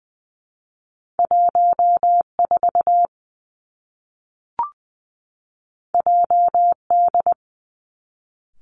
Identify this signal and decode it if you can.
Morse code, heard as a keyed tone, about 19 wpm, 700 Hz. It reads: 1 4 J D
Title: Two Challenging Letter-Number Combinations 20wpm